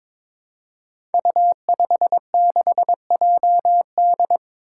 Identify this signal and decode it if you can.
Morse code, heard as a keyed tone, about 22 wpm, 700 Hz. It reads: U56JD
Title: Five Challenging Letter-Number Combinations 22wpm